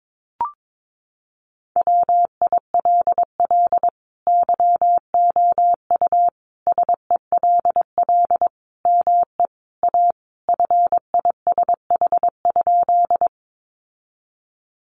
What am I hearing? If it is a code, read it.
Morse code, heard as a keyed tone, about 22 wpm, 700 Hz. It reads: WILL YOU SELL ME A FISH?
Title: Sentences from Top 1100 Words 22wpm